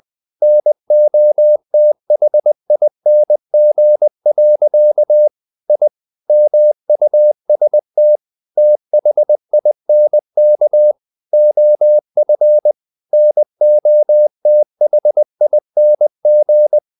Morse code, heard as a keyed tone, about 20 wpm, 600 Hz. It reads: NOTHING. I MUST THINK OF NOTHING
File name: the old man and the sea part 2 20wpm